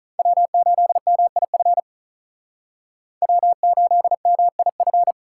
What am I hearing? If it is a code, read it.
Morse code, heard as a keyed tone, about 45 wpm, 700 Hz. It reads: W8MIF W8MIF